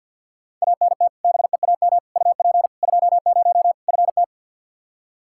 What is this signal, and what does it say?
Morse code, heard as a keyed tone, about 52 wpm, 700 Hz. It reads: ANT BEAM UP 20 FT
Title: Practice QSO - Exchange 2 - Info - 50wpm